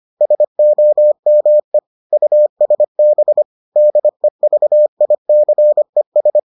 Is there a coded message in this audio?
Yes: Morse code from 0.2 to 6.4 s, heard as a keyed tone, about 25 wpm, 600 Hz. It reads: SOME USB DEVICES